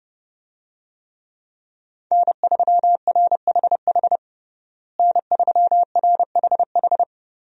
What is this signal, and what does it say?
Morse code, heard as a keyed tone, about 30 wpm, 700 Hz. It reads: N3RHH N3RHH
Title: USA Call Signs - 5 Characters - Repeated 2x - 30wpm